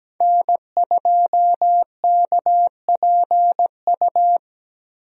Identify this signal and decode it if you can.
Morse code, heard as a keyed tone, about 17 wpm, 700 Hz. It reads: N2KPU
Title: USA Call Signs - 5 Characters - 17wpm